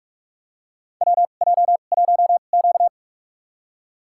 Morse code, heard as a keyed tone, about 47 wpm, spaced as for 33 wpm, 700 Hz. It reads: WJ1Q
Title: USA Call Signs - 2x Character Spacing - 45wpm